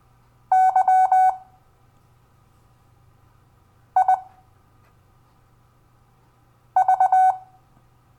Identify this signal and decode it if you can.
Morse code, heard as a keyed tone, about 20 wpm, spaced as for 3 wpm, 750 Hz. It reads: YIV
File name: cw / Lesson12 YV